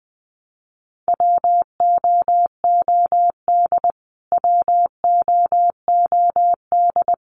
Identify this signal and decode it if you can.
Morse code, heard as a keyed tone, about 20 wpm, 700 Hz. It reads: WOOD WOOD
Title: Top 500 Words Repeated 20wpm